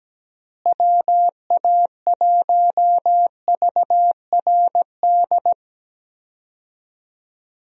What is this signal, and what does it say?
Morse code, heard as a keyed tone, about 17 wpm, 700 Hz. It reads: WA1VRD
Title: USA Call Signs - 6 Characters - Repeated 3x - 17wpm